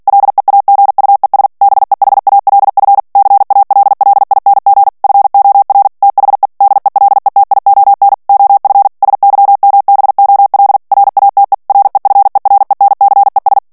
Morse code, heard as a keyed tone, about 48 wpm, 800 Hz. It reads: LEAGUES BEHALF CALLING FOR THE DELETION OF SYMBOL RATE REFERENCES
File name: ANGLAIS texte clair 3 vitesse 45 mots minute